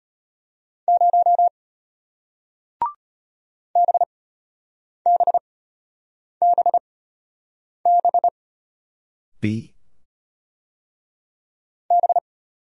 Morse code, heard as a keyed tone, about 34 wpm, spaced as for 8 wpm, 700 Hz. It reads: 0 BBBB B